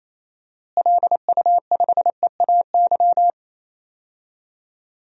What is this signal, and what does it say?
Morse code, heard as a keyed tone, about 28 wpm, 700 Hz. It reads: LU5EAY
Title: DXCC-100 Call Signs - 6 Characters - Repeated 3x - 28wpm